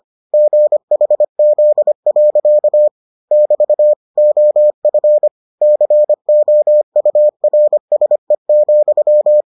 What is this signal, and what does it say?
Morse code, heard as a keyed tone, about 25 wpm, 600 Hz. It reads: GHZ. <BT> OF COURSE,